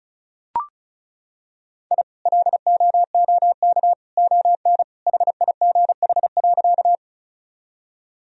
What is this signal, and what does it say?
Morse code, heard as a keyed tone, about 36 wpm, 700 Hz. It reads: I LOOK ON HIGH.
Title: Sentences from Top 1000 Words - Review Part 1 - 35wpm